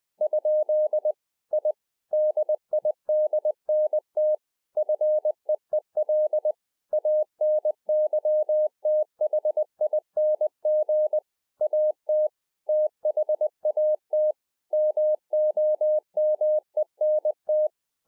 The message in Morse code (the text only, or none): ? I DIDNT FEEL ANYTHING AT THAT MOMENT